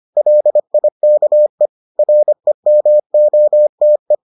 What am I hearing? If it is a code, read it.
Morse code, heard as a keyed tone, about 25 wpm, 600 Hz. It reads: LIKE REMOTE